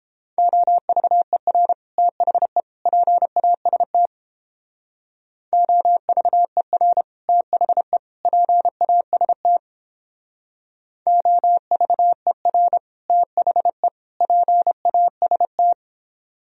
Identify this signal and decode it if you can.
Morse code, heard as a keyed tone, about 31 wpm, 700 Hz. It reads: OVER THE PAST OVER THE PAST OVER THE PAST